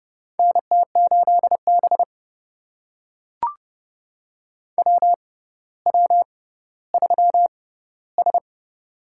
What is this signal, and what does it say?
Morse code, heard as a keyed tone, about 30 wpm, spaced as for 15 wpm, 700 Hz. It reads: NT8B WW3S
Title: USA Call Signs - 4 Characters - 6x Character Spacing - 30wpm